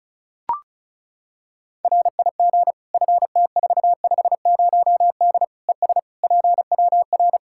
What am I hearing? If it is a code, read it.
Morse code, heard as a keyed tone, about 34 wpm, 700 Hz. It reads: RIG FT450D ES PWR
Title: K1USN SST QSO Practice 35wpm